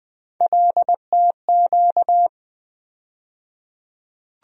Morse code, heard as a keyed tone, about 20 wpm, 700 Hz. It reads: LTQ
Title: USA Call Sign Postfixes - 3 Letters 20wpm